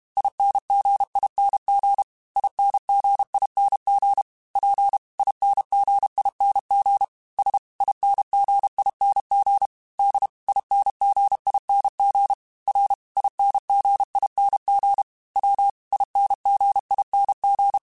Morse code, heard as a keyed tone, about 32 wpm, 800 Hz. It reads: INGING INGING P INGING S INGING D INGING R INGING W INGING
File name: Special INGs-mp3s / Special INGs-mp3s / Dble Ings sp @30